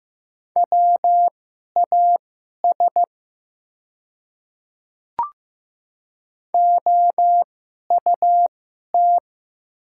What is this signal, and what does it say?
Morse code, heard as a keyed tone, about 15 wpm, spaced as for 11 wpm, 690 Hz. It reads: WAS OUT
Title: 2x Character Spacing - Top 100 Words - 15wpm